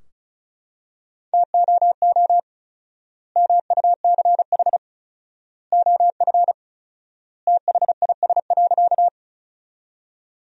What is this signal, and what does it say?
Morse code, heard as a keyed tone, about 36 wpm, 700 Hz. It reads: TOO MUCH OF THIS.
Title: 4x Word-Spacing - Sentences from Top 200 Words - 35wpm